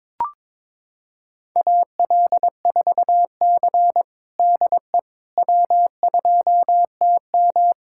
Morse code, heard as a keyed tone, about 22 wpm, 700 Hz. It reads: AL4C DE W2TM
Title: USA Callsign DE Callsign - 4 Characters - 22wpm